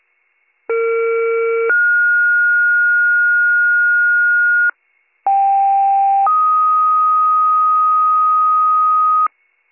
Two-tone page, heard as a one-second tone, then a three-second tone, about 460 then 1430 Hz, about 790 then 1250 Hz, with noise. It is heard from 0.7 to 9.3 s.